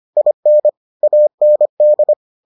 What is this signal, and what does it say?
Morse code, heard as a keyed tone, about 25 wpm, 600 Hz. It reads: IN AND